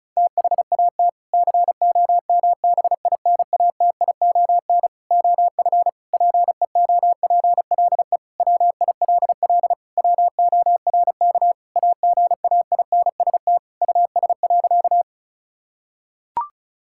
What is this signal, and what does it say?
Morse code, heard as a keyed tone, about 36 wpm, 700 Hz. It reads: THAT COMBINATION OF PEOPLE WILL WORK AGAINST US.
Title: Sentences from Top 1000 Words - Review Part 6 - 35wpm